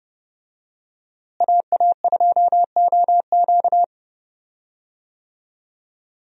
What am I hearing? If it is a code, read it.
Morse code, heard as a keyed tone, about 30 wpm, 700 Hz. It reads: AA2OQ
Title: USA Call Signs - 5 Characters - Repeated 2x - 30wpm